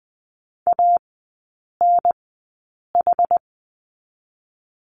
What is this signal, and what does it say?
Morse code, heard as a keyed tone, about 20 wpm, spaced as for 8 wpm, 690 Hz. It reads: ANH
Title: 159 Lesson 26e - Letter g - Three Characters 20wpm